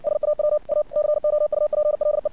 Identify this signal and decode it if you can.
Morse code, heard as a keyed tone, about 64 wpm, 600 Hz. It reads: SAW A COUPLE
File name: archived / CRSnet